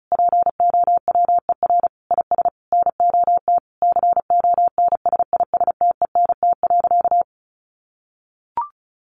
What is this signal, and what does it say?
Morse code, heard as a keyed tone, about 34 wpm, 700 Hz. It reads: POWER IS NOT CONSISTENT.
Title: Sentences from Top 1500 Words 35wpm